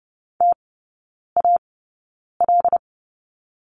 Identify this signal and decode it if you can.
Morse code, heard as a keyed tone, about 30 wpm, spaced as for 9 wpm, 690 Hz. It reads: TAL
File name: Lesson 16e - Letter c - Three Characters 30wpm